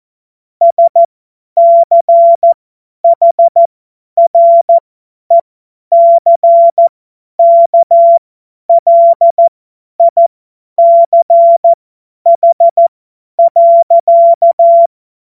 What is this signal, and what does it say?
Morse code, heard as a keyed tone, about 14 wpm, spaced as for 10 wpm, 680 Hz. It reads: SCHRECKLICH.
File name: Morsekurs / Witze / Witz 26 März 2024 50 F70